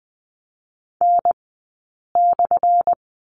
Morse code, heard as a keyed tone, about 20 wpm, 700 Hz. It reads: N /
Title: 177 Lesson 29d - Slash - Two Characters 20wpm